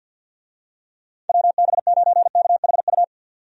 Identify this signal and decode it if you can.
Morse code, heard as a keyed tone, about 50 wpm, 700 Hz. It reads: WB9KHU
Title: USA Call Signs - 6 Characters - 50wpm